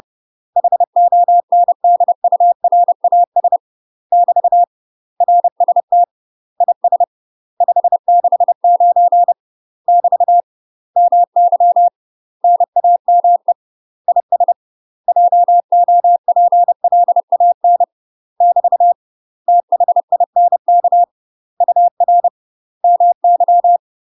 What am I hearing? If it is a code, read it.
Morse code, heard as a keyed tone, about 30 wpm, 700 Hz. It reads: HONDURAS <BT> RST IS 569 <BT> MY NAME IS JOPLAN <BT> THINK UR MY
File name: UK MASTER QSOs TRAINING / 30WPM / UK MASTER QSOs 0050